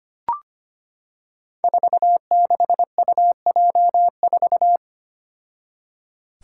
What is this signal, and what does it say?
Morse code, heard as a keyed tone, about 25 wpm, 700 Hz. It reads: 46UJ4
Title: Five Challenging Letter-Number Combinations 25wpm